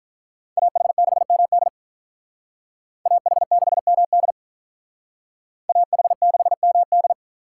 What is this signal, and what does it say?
Morse code, heard as a keyed tone, about 43 wpm, 700 Hz. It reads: AH6MD AH6MD AH6MD
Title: Speed Racing - USA 2x2 Call Signs 35wpm